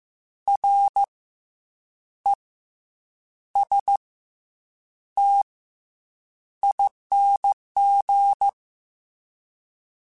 Morse code, heard as a keyed tone, about 15 wpm, 800 Hz. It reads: R E S T ING
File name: Special INGs / Special INGs-mp3s / SlowToFaster ing @13